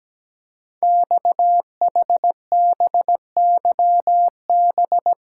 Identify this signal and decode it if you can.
Morse code, heard as a keyed tone, about 17 wpm, 700 Hz. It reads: XHBYB
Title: Code Groups - 5 Letters - 17wpm